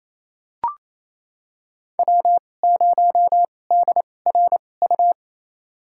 Morse code, heard as a keyed tone, about 28 wpm, spaced as for 20 wpm, 700 Hz. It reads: W0DRU
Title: USA Call Signs - 5 Characters - 2x Character Spacing - 28wpm